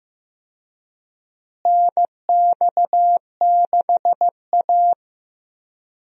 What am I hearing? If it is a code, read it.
Morse code, heard as a keyed tone, about 15 wpm, 700 Hz. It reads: NX6A
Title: USA Call Signs - 4 Characters - Repeated 3x - 15wpm